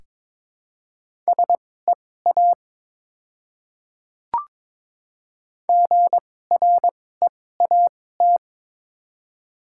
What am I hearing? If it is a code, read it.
Morse code, heard as a keyed tone, about 23 wpm, spaced as for 16 wpm, 700 Hz. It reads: SEA GREAT
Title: 2x Character Spacing - Top 500 Words - 22wpm